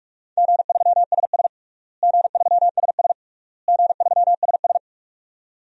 Morse code, heard as a keyed tone, about 42 wpm, 700 Hz. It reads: G3SS G3SS G3SS
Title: DXCC-100 Call Signs - 4 Characters - Repeated 3x - 45wpm